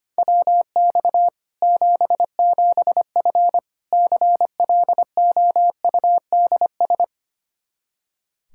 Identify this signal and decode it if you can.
Morse code, heard as a keyed tone, about 25 wpm, 700 Hz. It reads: WX 77F CLOUDS
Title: Practice QSO - Exchange 2 - Info - 25wpm